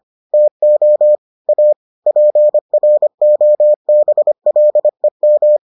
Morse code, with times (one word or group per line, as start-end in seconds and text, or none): 0.3-1.1 TO
1.5-1.7 A
2.1-5.6 PROBLEM